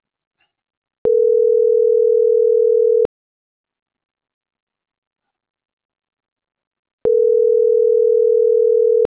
Ringback tone (North America), 440 + 480 Hz.